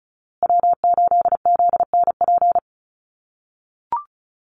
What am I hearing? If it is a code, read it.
Morse code, heard as a keyed tone, about 34 wpm, 700 Hz. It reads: W8ZNP
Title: USA Call Signs Repeated 3x 35wpm